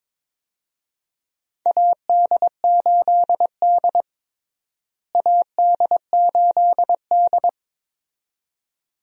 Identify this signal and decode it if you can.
Morse code, heard as a keyed tone, about 22 wpm, 700 Hz. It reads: AD8D AD8D
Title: USA Call Signs - 4 Characters - Repeated 2x - 22wpm